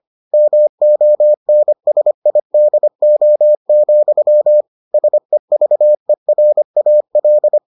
Morse code, heard as a keyed tone, about 25 wpm, 600 Hz. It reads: MONSIDO, SEVERAL